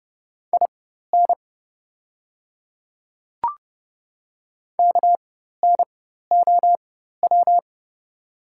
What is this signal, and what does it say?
Morse code, heard as a keyed tone, about 30 wpm, spaced as for 14 wpm, 700 Hz. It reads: IN KNOW
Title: 4x Character Spacing - Top 100 Words - 30wpm